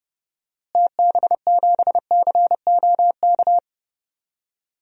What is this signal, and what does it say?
Morse code, heard as a keyed tone, about 30 wpm, 700 Hz. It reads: TB7COK
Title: DXCC-100 Call Signs - 6 Characters - Repeated 3x - 30wpm